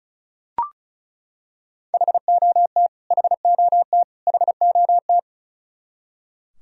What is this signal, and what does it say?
Morse code, heard as a keyed tone, about 34 wpm, 700 Hz. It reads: HOT HOT HOT
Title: Top 100 Words Repeated 35wpm